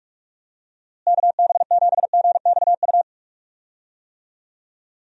Morse code, heard as a keyed tone, about 47 wpm, 700 Hz. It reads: KB7GXU
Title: USA Call Signs - 6 Characters - Repeated 3x - 45wpm